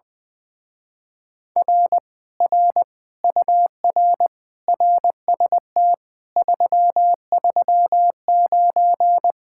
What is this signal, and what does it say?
Morse code, heard as a keyed tone, about 21 wpm, 700 Hz. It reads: R R UR RST 339
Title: Speed-Racing Common QSO Phrases - 20wpm